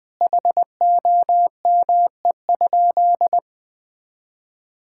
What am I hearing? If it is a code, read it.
Morse code, heard as a keyed tone, about 20 wpm, 700 Hz. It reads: HOME?
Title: Sentences from Top 1600 Words 20wpm